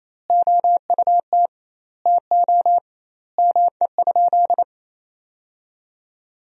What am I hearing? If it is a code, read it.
Morse code, heard as a keyed tone, about 28 wpm, 700 Hz. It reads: OUT TO ME?
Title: 2x Word-Spacing - Sentences from Top 200 Words - 28wpm